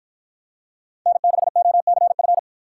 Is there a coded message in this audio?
Yes: Morse code from 1.1 to 2.4 s, heard as a keyed tone, about 50 wpm, 700 Hz. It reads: N6OCF